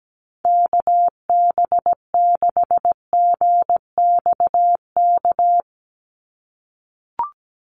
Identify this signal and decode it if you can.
Morse code, heard as a keyed tone, about 17 wpm, 700 Hz. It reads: KB6GXK
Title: USA Call Signs 17wpm